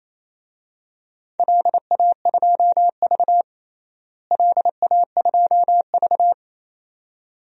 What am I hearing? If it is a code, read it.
Morse code, heard as a keyed tone, about 27 wpm, 700 Hz. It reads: LA2V LA2V
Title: DXCC-100 Call Signs - 4 Characters - Repeated 2x - 28wpm